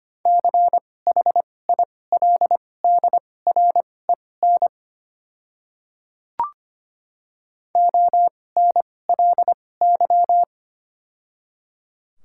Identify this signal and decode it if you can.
Morse code, heard as a keyed tone, about 25 wpm, spaced as for 18 wpm, 700 Hz. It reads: CHILDREN ONLY
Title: 2x Character Spacing - Top 300 Words - 25wpm